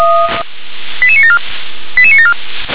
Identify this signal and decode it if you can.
Five-tone selective call (ZVEI), heard as quick runs of five tones, tones of about 70 ms. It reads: 88974 88974
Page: SDR/ZVEI